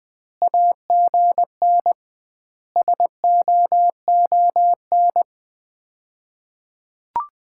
Morse code, heard as a keyed tone, about 19 wpm, 700 Hz. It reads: AGN SOON
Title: Common QSO Phrases - 2x Word-Spacing - 20wpm